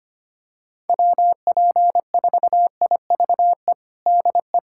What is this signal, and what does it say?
Morse code, heard as a keyed tone, about 25 wpm, 700 Hz. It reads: WP4IVE DE
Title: USA Callsign DE Callsign 25wpm